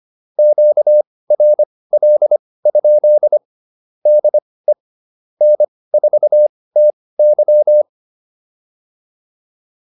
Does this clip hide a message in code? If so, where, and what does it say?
Yes: Morse code from 0.4 to 7.8 s, heard as a keyed tone, about 25 wpm, spaced as for 18 wpm, 600 Hz. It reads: QRL? DE N4TY